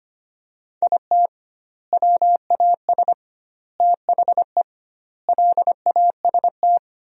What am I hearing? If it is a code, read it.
Morse code, heard as a keyed tone, about 25 wpm, 700 Hz. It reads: IT WAS THE LAST
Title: 2x Word-Spacing - Sentences from Top 300 Words - 25wpm